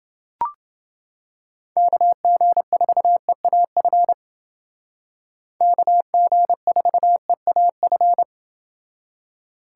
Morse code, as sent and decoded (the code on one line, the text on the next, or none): -.- --. ....- . .- ..-. / -.- --. ....- . .- ..-.
KG4EAF KG4EAF